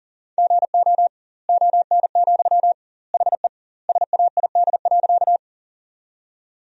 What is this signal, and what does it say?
Morse code, heard as a keyed tone, about 40 wpm, 700 Hz. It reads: GO ON, HE SAID.
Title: 2x Word-Spacing - Sentences from Top 100 Words - 40wpm